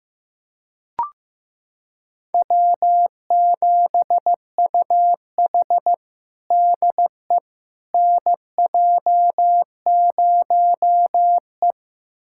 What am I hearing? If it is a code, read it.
Morse code, heard as a keyed tone, about 15 wpm, 700 Hz. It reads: W7UH DE NJ0E
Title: USA Callsign DE Callsign - 4 Characters - 15wpm